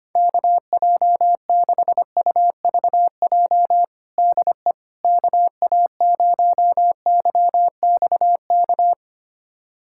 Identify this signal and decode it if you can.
Morse code, heard as a keyed tone, about 25 wpm, 700 Hz. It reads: KJ6UVJ DE KA0YXK
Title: USA Callsign DE Callsign - 6 Characters - 25wpm